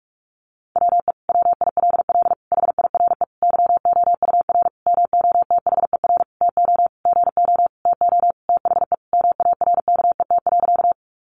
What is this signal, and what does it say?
Morse code, heard as a keyed tone, about 44 wpm, 700 Hz. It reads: WE WILL HIRE YOUR MOTHER TO GO TO THE MARKET.
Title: Sentences from Top 1400 Words 45wpm